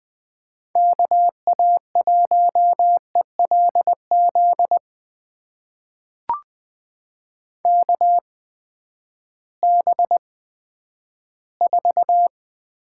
Morse code, heard as a keyed tone, about 20 wpm, spaced as for 11 wpm, 700 Hz. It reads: KA1ELZ KB4
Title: USA Call Signs - 8x Character Spacing - 20wpm